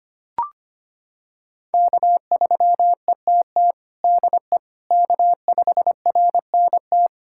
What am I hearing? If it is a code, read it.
Morse code, heard as a keyed tone, about 25 wpm, 700 Hz. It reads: K3ETT DE K5RNT
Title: USA Callsign DE Callsign - 5 Characters - 25wpm